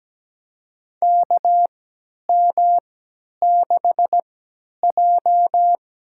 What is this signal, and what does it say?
Morse code, heard as a keyed tone, about 17 wpm, spaced as for 10 wpm, 700 Hz. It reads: KM6J
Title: USA Call Signs - 4 Characters - 3x Character Spacing - 17wpm